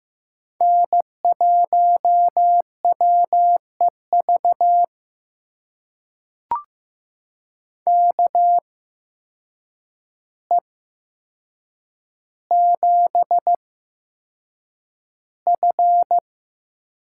Morse code, heard as a keyed tone, about 15 wpm, spaced as for 7 wpm, 700 Hz. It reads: N1WEV KE7F
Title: USA Call Signs - 8x Character Spacing - 15wpm